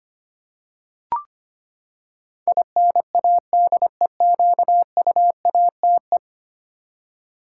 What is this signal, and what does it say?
Morse code, heard as a keyed tone, about 25 wpm, 700 Hz. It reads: INADEQUATE
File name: Ten Letter Words - 25wpm